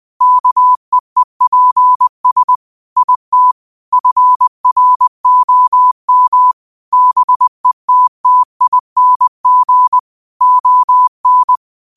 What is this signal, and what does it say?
Morse code, heard as a keyed tone, about 20 wpm, 1000 Hz. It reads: KEEPS IT FROM BETTING ON
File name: qotdfiles / 20wpm